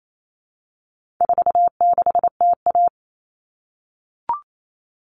Morse code, heard as a keyed tone, about 28 wpm, 700 Hz. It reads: 46TA
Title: All Cut Numbers - 4-Digit 28wpm